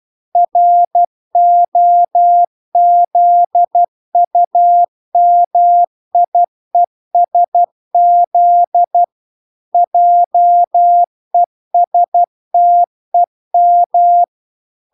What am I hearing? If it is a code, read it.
Morse code, heard as a keyed tone, about 12 wpm, 700 Hz. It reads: ROZUMIESZ JESTEM
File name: recordings / 12wpm0000